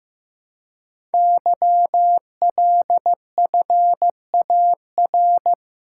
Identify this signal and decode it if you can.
Morse code, heard as a keyed tone, about 15 wpm, 700 Hz. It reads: YLFAR
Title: Code Groups - 5 Letters - 15wpm